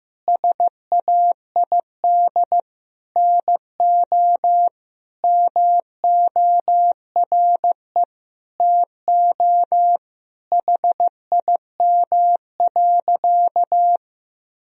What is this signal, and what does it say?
Morse code, heard as a keyed tone, about 15 wpm, 700 Hz. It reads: SAID NO MORE TO HIM.
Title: Sentences from Top 1000 Words - Review Part 1 - 15wpm